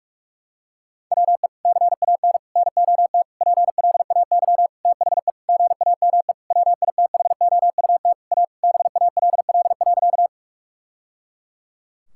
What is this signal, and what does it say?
Morse code, heard as a keyed tone, about 46 wpm, 700 Hz. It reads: WE CAN NOT PLAY THE GAME WITHOUT A BALL.
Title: Sentences from Top 1000 Words - Review Part 6 - 45wpm